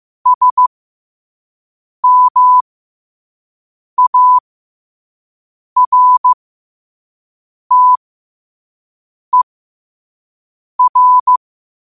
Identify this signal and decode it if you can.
Morse code, heard as a keyed tone, about 15 wpm, spaced as for 5 wpm, 1000 Hz. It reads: SMARTER
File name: qotdfiles / 5wpm